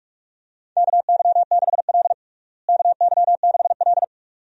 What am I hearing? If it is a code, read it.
Morse code, heard as a keyed tone, about 46 wpm, 700 Hz. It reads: KY6L KY6L
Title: USA Call Signs - 4 Characters - Repeated 2x - 45wpm